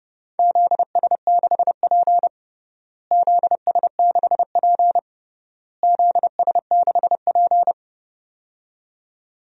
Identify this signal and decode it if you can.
Morse code, heard as a keyed tone, about 30 wpm, 700 Hz. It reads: ZS6P ZS6P ZS6P
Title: DXCC-100 Call Signs - 4 Characters - Repeated 3x - 30wpm